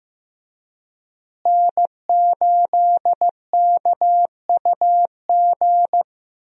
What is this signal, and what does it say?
Morse code, heard as a keyed tone, about 15 wpm, 700 Hz. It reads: N8KUG
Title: USA Call Signs - 5 Characters - Repeated 3x - 15wpm